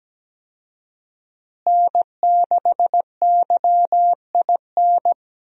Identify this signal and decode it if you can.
Morse code, heard as a keyed tone, about 17 wpm, 700 Hz. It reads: N6YIN
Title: USA Call Signs - 5 Characters - Repeated 2x - 17wpm